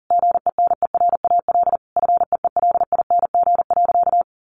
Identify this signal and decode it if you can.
Morse code, heard as a keyed tone, about 40 wpm, 700 Hz. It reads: GENERAL FEELING.
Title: Sentences from Top 1400 Words 40wpm